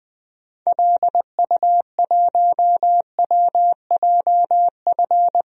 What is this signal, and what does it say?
Morse code, heard as a keyed tone, about 20 wpm, 700 Hz. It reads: LU1WJF
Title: DXCC-100 Call Signs - 6 Characters - Repeated 2x - 20wpm